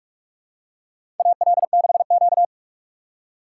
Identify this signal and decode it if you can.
Morse code, heard as a keyed tone, about 47 wpm, 700 Hz. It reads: AL6Q
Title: USA Call Signs - 4 Characters - 2x Character Spacing - 45wpm